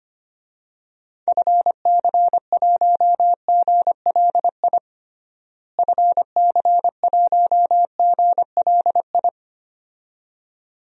Morse code, heard as a keyed tone, about 25 wpm, 700 Hz. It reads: FC1GLI FC1GLI
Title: DXCC-100 Call Signs - 6 Characters - Repeated 2x - 25wpm